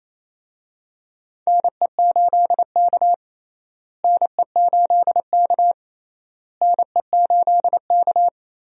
Morse code, heard as a keyed tone, about 28 wpm, 700 Hz. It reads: NE8K NE8K NE8K
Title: USA Call Signs - 4 Characters - Repeated 3x - 28wpm